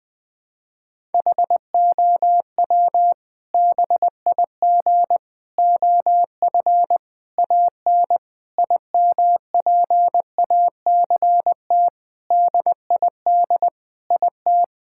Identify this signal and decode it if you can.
Morse code, heard as a keyed tone, about 20 wpm, 700 Hz. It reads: HOW BIG OF AN IMPACT DID IT
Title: Sentences from Top 1000 Words - Review Part 6 - 20wpm